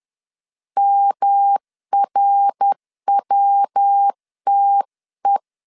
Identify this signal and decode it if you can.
Morse code, heard as a keyed tone, about 10 wpm, 800 Hz. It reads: MRWTE
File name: 10wpm